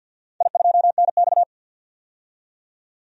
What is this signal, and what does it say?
Morse code, heard as a keyed tone, about 47 wpm, 700 Hz. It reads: I2NX
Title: DXCC-100 Call Signs - 4 Characters - 50wpm